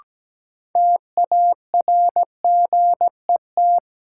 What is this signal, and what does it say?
Morse code, heard as a keyed tone, about 17 wpm, 700 Hz. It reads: TARGET
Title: Sentences from Top 1200 Words - New Word Drill - 17wpm